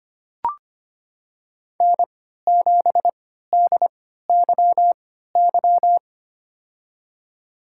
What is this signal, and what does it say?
Morse code, heard as a keyed tone, about 25 wpm, spaced as for 14 wpm, 700 Hz. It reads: N7DYY
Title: USA Call Signs - 5 Characters - 3x Character Spacing - 25wpm